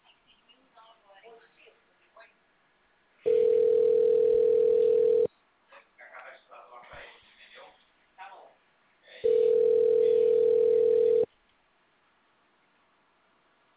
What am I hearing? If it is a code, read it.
Ringback tone (North America), 440 + 480 Hz, with speech.
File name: out- -2012- - - .30315